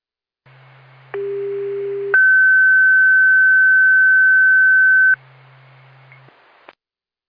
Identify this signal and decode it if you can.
Two-tone page, heard as a one-second tone, then a three-second tone, about 400 then 1550 Hz. It is heard from 1.1 to 5.1 s.